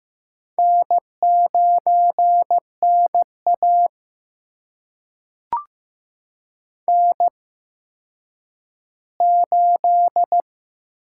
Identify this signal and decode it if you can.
Morse code, heard as a keyed tone, about 15 wpm, spaced as for 9 wpm, 700 Hz. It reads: N9NA N8
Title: USA Call Signs - 4 Characters - 8x Character Spacing - 15wpm